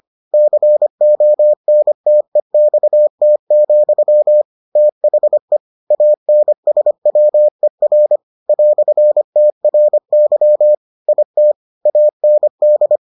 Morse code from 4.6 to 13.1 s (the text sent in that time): THE ANSWER "TRY IT AND